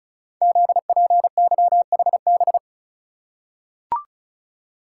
Morse code, heard as a keyed tone, about 36 wpm, 690 Hz. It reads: ZPYHB
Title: Code Groups - 5 Letter:Numbers - 35wpm